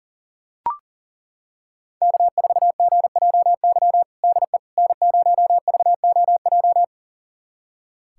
Morse code, heard as a keyed tone, about 40 wpm, 700 Hz. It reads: K4GJY DE N0VOJ
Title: USA Callsign DE Callsign - 5 Characters - 40wpm